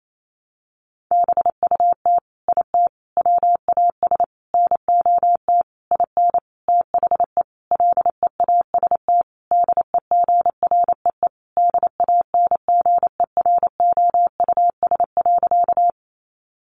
Morse code, heard as a keyed tone, about 28 wpm, 700 Hz. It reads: BUT IT WAS NOT IN THE LEAST DEGREE DANGEROUS.
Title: Sentences from Top 1500 Words 28wpm